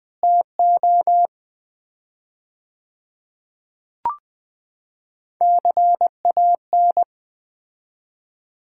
Morse code, heard as a keyed tone, about 20 wpm, 700 Hz. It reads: TO CAN
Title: 4x Word-Spacing - Sets of 3 Words - 20wpm